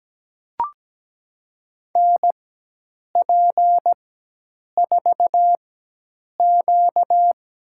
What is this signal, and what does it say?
Morse code, heard as a keyed tone, about 17 wpm, spaced as for 8 wpm, 700 Hz. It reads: NP4Q
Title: USA Call Signs - 4x Character Spacing - 17wpm